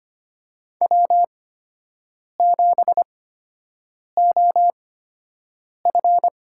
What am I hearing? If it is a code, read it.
Morse code, heard as a keyed tone, about 25 wpm, spaced as for 7 wpm, 700 Hz. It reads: W7OF
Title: USA Call Signs - 8x Character Spacing - 25wpm